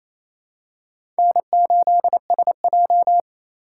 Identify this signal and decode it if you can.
Morse code, heard as a keyed tone, about 28 wpm, 700 Hz. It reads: N8SJ
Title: USA Call Signs - 4 Characters - 3x Character Spacing - 28wpm